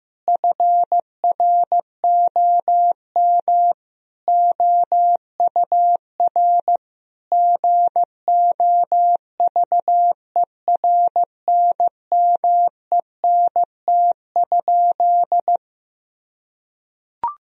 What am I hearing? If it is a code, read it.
Morse code, heard as a keyed tone, about 14 wpm, 700 Hz. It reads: FROM OUR GOVERNMENT?
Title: Sentences from Top 1000 Words - Review Part 3 - 15wpm